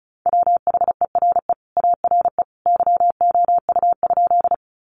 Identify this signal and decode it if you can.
Morse code, heard as a keyed tone, about 34 wpm, 700 Hz. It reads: WHERE ARE YOU?
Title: Sentences from Top 200 Words 35wpm